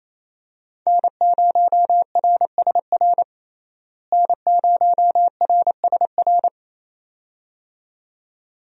Morse code, heard as a keyed tone, about 28 wpm, 700 Hz. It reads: N0RSR N0RSR
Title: USA Call Signs - 5 Characters - Repeated 2x - 28wpm